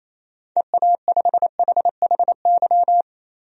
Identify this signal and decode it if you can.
Morse code, heard as a keyed tone, about 28 wpm, 700 Hz. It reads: EA5HHY
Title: DXCC-100 Call Signs - 6 Characters - 28wpm